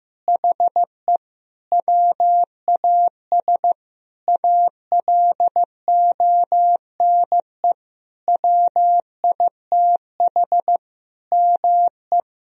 Morse code, heard as a keyed tone, about 15 wpm, 700 Hz. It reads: HE WAS ALONE WITH ME
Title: Sentences from Top 1000 Words - Review Part 4 - 15wpm